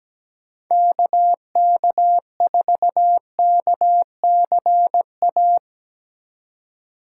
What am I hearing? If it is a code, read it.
Morse code, heard as a keyed tone, about 18 wpm, 700 Hz. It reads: KK4KCA
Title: Speed Racing - USA Call Signs 17wpm